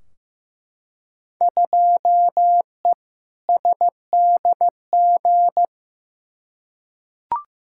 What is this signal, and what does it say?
Morse code, heard as a keyed tone, about 15 wpm, 700 Hz. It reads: 2E SDG
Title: Field Day Exchanges - 15wpm